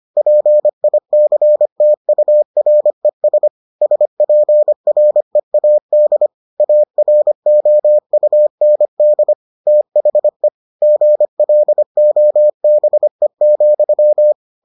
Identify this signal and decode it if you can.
Morse code, heard as a keyed tone, about 25 wpm, 600 Hz. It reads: PICTURES SPREAD AROUND THE GLOBE,